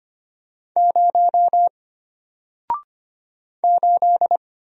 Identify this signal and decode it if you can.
Morse code, heard as a keyed tone, about 25 wpm, 700 Hz. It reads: 0 8
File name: Single-Letter-Number-25wpm